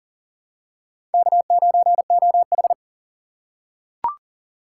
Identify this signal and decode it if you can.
Morse code, heard as a keyed tone, about 40 wpm, 700 Hz. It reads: K9OH
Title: USA Call Signs - 4 Characters - 4x Character Spacing - 40wpm